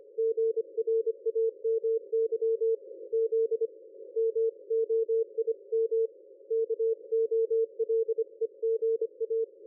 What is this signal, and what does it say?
Morse code, heard as a keyed tone, about 25 wpm, 450 Hz. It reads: GRAMY Z MOIM KOLEGA
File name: dlazaawansowanych / nagraniazpasma / ExpertSDR2 line out 23.01.2020 14.12.00